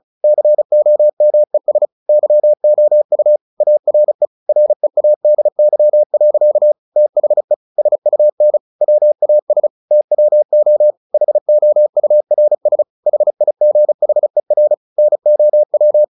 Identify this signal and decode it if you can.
Morse code, heard as a keyed tone, about 34 wpm, 600 Hz. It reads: COMES YOU ARE READY. THE SUN WAS TWO HOURS HIGHER NOW
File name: the old man and the sea / the old man and the sea 1 35wpm